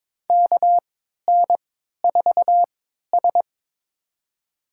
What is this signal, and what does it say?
Morse code, heard as a keyed tone, about 22 wpm, spaced as for 13 wpm, 700 Hz. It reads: KN4S
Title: USA Call Signs - 4 Characters - 3x Character Spacing - 22wpm